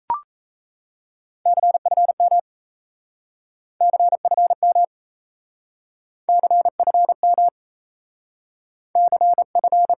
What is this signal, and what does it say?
Morse code, heard as a keyed tone, about 37 wpm, 700 Hz. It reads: CFM CFM CFM CF